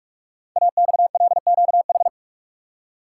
Morse code, heard as a keyed tone, about 44 wpm, 690 Hz. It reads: AXLQH
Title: Code Groups - 5 Letters - 45wpm